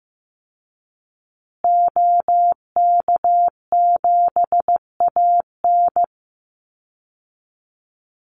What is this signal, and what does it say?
Morse code, heard as a keyed tone, about 15 wpm, 700 Hz. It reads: OK7AN
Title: Call Signs Repeated 2x - 15wpm